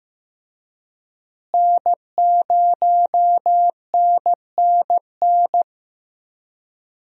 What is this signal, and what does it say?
Morse code, heard as a keyed tone, about 15 wpm, 700 Hz. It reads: N0NNN
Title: USA Call Signs - 5 Characters - Repeated 3x - 15wpm